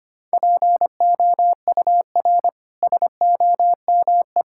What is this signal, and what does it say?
Morse code, heard as a keyed tone, about 25 wpm, 700 Hz. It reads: POUR SOME
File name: Easy Sentences with Hints - 25wpm